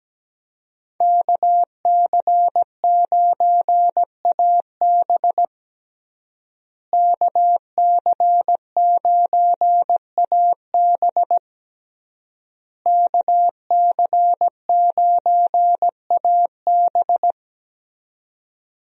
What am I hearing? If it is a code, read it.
Morse code, heard as a keyed tone, about 17 wpm, 700 Hz. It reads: KC9AB KC9AB KC9AB
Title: USA Call Signs - 5 Characters - Repeated 3x - 17wpm